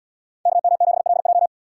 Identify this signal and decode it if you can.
Morse code, heard as a keyed tone, about 74 wpm, 700 Hz. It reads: KM7LJ